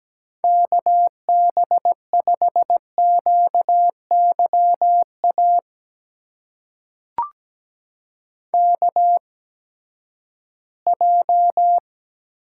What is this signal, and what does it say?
Morse code, heard as a keyed tone, about 17 wpm, spaced as for 12 wpm, 700 Hz. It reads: KB5QYA KJ
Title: USA Call Signs - 6 Characters - 8x Character Spacing - 17wpm